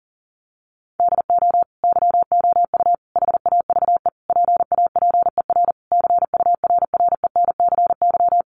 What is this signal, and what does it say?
Morse code, heard as a keyed tone, about 40 wpm, 700 Hz. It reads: DO YOU HAVE PAPER CURRENCY
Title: Sentences from Top 1500 Words 40wpm